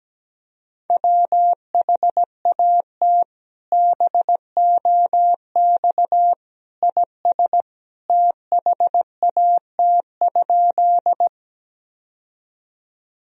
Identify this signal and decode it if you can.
Morse code, heard as a keyed tone, about 17 wpm, 700 Hz. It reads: WHAT BOX IS THAT?
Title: Sentences from Top 1000 Words - Review Part 2 - 17wpm